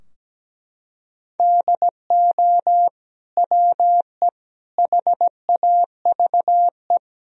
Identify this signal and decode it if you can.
Morse code, heard as a keyed tone, about 17 wpm, 700 Hz. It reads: DO WE HAVE
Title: Sentences from Top 1000 Words - Review Part 3 - 17wpm